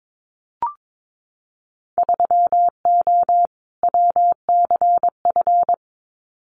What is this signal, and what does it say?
Morse code, heard as a keyed tone, about 22 wpm, 700 Hz. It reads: 3O WCF
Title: Winter Field Day Exchanges 22wpm